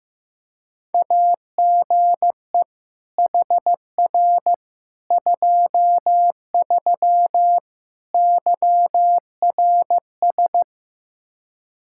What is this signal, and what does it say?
Morse code, heard as a keyed tone, about 15 wpm, 700 Hz. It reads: AGE HR 23 YRS
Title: K1USN SST QSO Practice 15wpm